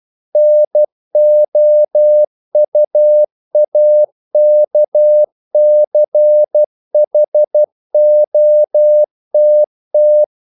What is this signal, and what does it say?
Morse code, heard as a keyed tone, about 12 wpm, 600 Hz. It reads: NOUAKCHOTT
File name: CW / Practice / NationalCapitals12